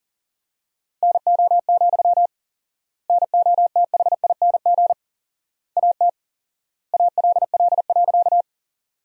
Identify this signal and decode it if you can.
Morse code, heard as a keyed tone, about 37 wpm, 700 Hz. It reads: NO, NOTHING AT ALL.
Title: 4x Word-Spacing - Sentences from Top 500 Words - 40wpm